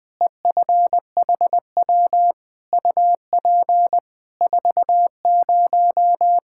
Morse code, heard as a keyed tone, about 20 wpm, 700 Hz. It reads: EFHW UP 40
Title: Practice QSO - Exchange 2 - Info - 20wpm